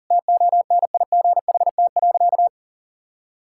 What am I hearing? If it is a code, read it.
Morse code, heard as a keyed tone, about 40 wpm, 700 Hz. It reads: TONIGHT.
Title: Sentences from Top 1000 Words - Review Part 5 - 40wpm